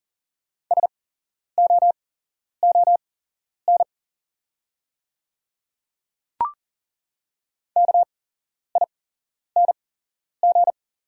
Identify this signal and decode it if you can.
Morse code, heard as a keyed tone, about 40 wpm, spaced as for 11 wpm, 690 Hz. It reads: SOON KING
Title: 8x Character Spacing - Top 500 Words - 40wpm